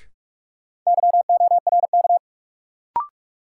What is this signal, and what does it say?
Morse code, heard as a keyed tone, about 41 wpm, 700 Hz. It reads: YORK
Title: Sets of 2 Words - New Word Drill - 45wpm